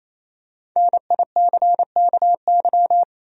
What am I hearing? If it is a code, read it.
Morse code, heard as a keyed tone, about 28 wpm, 700 Hz. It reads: NICKY
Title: USA Common Names - Max 5 Letters - 28wpm